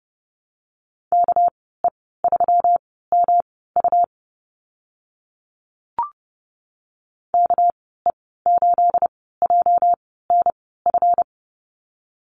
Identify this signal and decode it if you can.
Morse code, heard as a keyed tone, about 30 wpm, spaced as for 17 wpm, 700 Hz. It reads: KE3MU KE8JNF
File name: Lesson 37g - Letter x - Call Signs 30wpm